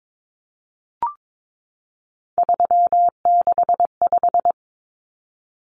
Morse code, heard as a keyed tone, about 22 wpm, 700 Hz. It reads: 365
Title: Common Cut Numbers - 3-Digit 22wpm